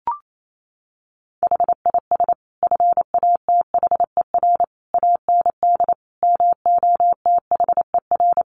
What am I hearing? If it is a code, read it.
Morse code, heard as a keyed tone, about 28 wpm, 700 Hz. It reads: HIS FATHER AND MOTHER
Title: Sentences from Top 1500 Words 28wpm